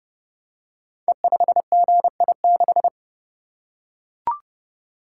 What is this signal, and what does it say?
Morse code, heard as a keyed tone, about 29 wpm, 700 Hz. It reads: E5GI6
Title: Code Groups - 5 Letter:Numbers - 30wpm